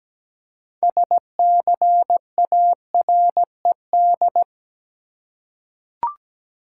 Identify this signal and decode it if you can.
Morse code, heard as a keyed tone, about 17 wpm, 700 Hz. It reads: SCARED
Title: Six Letter Words - 17wpm